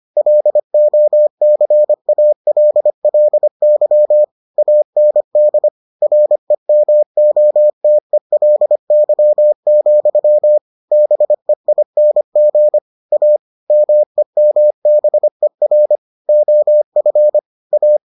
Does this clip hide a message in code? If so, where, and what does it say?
Yes: Morse code from 0.2 to 18.0 s, heard as a keyed tone, about 25 wpm, 600 Hz. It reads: LOCALLY AND REMOTELY, BEING A MEMBER OF A